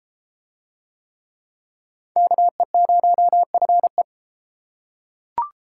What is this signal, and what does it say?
Morse code, heard as a keyed tone, about 30 wpm, 700 Hz. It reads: KE0FE